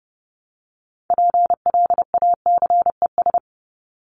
Morse code, heard as a keyed tone, about 30 wpm, 700 Hz. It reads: PLACES
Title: ES:S Suffix - Top 100 Words with Base Words 30wpm